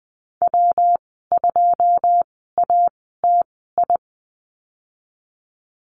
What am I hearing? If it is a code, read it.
Morse code, heard as a keyed tone, about 20 wpm, spaced as for 15 wpm, 700 Hz. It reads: W2ATI
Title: 128 Lesson 21g - Number 6 - Call Signs 20wpm